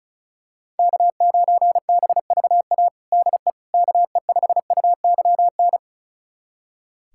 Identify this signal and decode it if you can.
Morse code, heard as a keyed tone, about 36 wpm, 700 Hz. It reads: K9BVA DE KE5UYN
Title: USA Callsign DE Callsign 35wpm